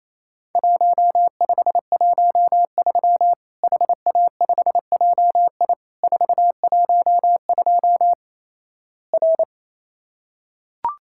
Morse code, heard as a keyed tone, about 27 wpm, 700 Hz. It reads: 1513 HA5JI 412 R
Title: Worked All Europe DX Contest - QTC Exchange - 28wpm